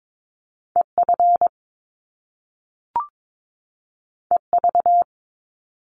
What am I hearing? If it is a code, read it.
Morse code, heard as a keyed tone, about 22 wpm, 700 Hz. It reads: EF EV
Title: Top Two Letter Combinations in Words 22wpm